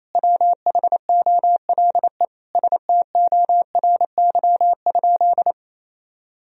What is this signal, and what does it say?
Morse code, heard as a keyed tone, about 28 wpm, 700 Hz. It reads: WHOLE STORY?
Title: Sentences from Top 400 Words and Abbreviations 28wpm